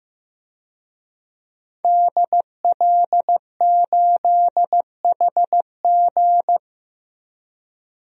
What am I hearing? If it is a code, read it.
Morse code, heard as a keyed tone, about 15 wpm, 700 Hz. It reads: DL8HG
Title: DXCC-100 Call Signs - 5 Characters - 15wpm